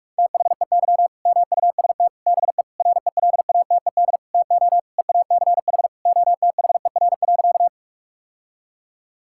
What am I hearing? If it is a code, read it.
Morse code, heard as a keyed tone, about 46 wpm, 700 Hz. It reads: THEY MUST BE RELATED TO EACH OTHER.
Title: Sentences from Top 1000 Words - Review Part 5 - 45wpm